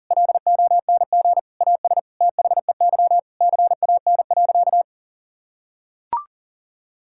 Morse code, heard as a keyed tone, about 40 wpm, 700 Hz. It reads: LONG AS THEY CAN.
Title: Sentences from Top 1000 Words - Review Part 5 - 40wpm